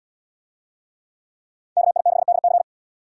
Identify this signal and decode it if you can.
Morse code, heard as a keyed tone, about 73 wpm, 700 Hz. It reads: KI6DC